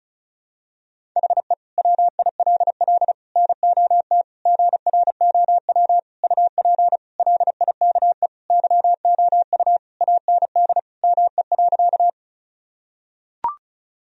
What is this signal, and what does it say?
Morse code, heard as a keyed tone, about 36 wpm, 700 Hz. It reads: HE WILL NOT GROW UP LIKE YOU AND ME.
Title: Sentences from Top 1000 Words - Review Part 1 - 35wpm